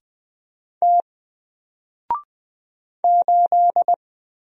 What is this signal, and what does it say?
Morse code, heard as a keyed tone, about 20 wpm, 700 Hz. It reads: T 8
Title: Single Character (Letter:Number:Punctuation:Prosign) Alt 20wpm